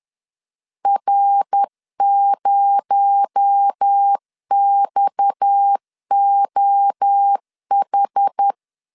Morse code, heard as a keyed tone, about 10 wpm, 800 Hz. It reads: R0XOH